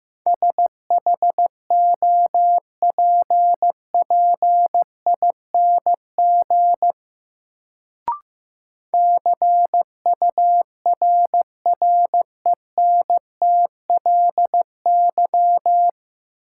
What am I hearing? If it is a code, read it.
Morse code, heard as a keyed tone, about 15 wpm, 700 Hz. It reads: SHOPPING CURRENTLY
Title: Sentences from Top 1000 Words - Review Part 7 - New Word Drill - 15wpm